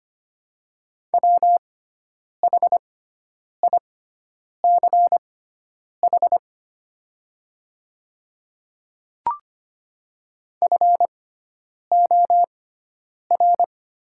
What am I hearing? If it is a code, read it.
Morse code, heard as a keyed tone, about 25 wpm, spaced as for 9 wpm, 690 Hz. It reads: WHICH FOR
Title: 6x Character Spacing - Top 200 Words - 25wpm